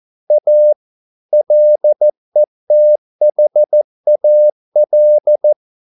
Morse code, heard as a keyed tone, about 14 wpm, 600 Hz. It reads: A LETHAL